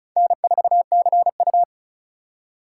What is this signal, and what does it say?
Morse code, heard as a keyed tone, about 36 wpm, 700 Hz. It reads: N4CU
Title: USA Call Signs - 4 Characters - 4x Character Spacing - 35wpm